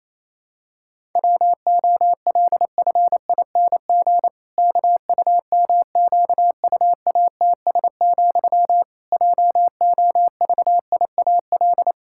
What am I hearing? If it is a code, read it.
Morse code, heard as a keyed tone, about 28 wpm, 700 Hz. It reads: WOLFING KUMQUATS, JOVIAL
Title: Pangram Challenge v2 - 28wpm